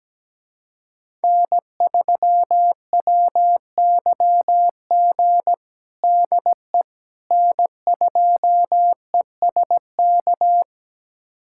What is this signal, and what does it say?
Morse code, heard as a keyed tone, about 17 wpm, 700 Hz. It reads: N3WYG DE N2ESK
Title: USA Callsign DE Callsign - 5 Characters - 17wpm